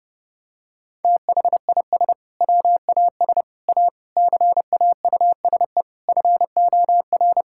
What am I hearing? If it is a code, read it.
Morse code, heard as a keyed tone, about 32 wpm, 700 Hz. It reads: THIS WAS A CAUSE FOR
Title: Sentences from Top 1600 Words 30wpm